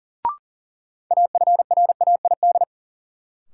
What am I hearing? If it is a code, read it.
Morse code, heard as a keyed tone, about 40 wpm, 700 Hz. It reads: AFRAID
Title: Sentences from Top 1500 Words - New Word Drill - 40wpm